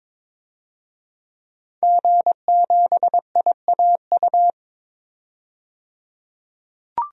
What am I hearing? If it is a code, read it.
Morse code, heard as a keyed tone, about 22 wpm, 700 Hz. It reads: G7IAU
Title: DXCC-100 Call Signs - 5 Characters - 22wpm